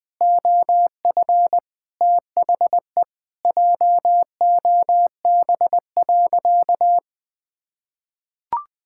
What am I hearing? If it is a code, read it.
Morse code, heard as a keyed tone, about 20 wpm, 700 Hz. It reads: OF THE JOB.
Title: Sentences from Top 1000 Words - Review Part 5 - 20wpm